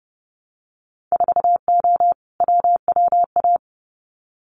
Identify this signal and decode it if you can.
Morse code, heard as a keyed tone, about 30 wpm, 700 Hz. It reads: 4O WWA
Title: Winter Field Day Exchanges 30wpm